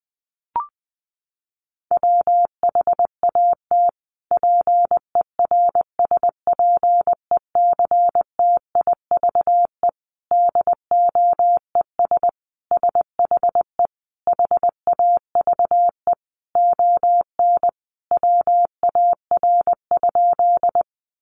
Morse code, heard as a keyed tone, about 20 wpm, 700 Hz. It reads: WHAT PERSPECTIVE DOES SHE HAVE ON WAR?
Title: Sentences from Top 1200 Words 20wpm